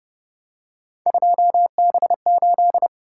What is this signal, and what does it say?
Morse code, heard as a keyed tone, about 28 wpm, 700 Hz. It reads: 2B8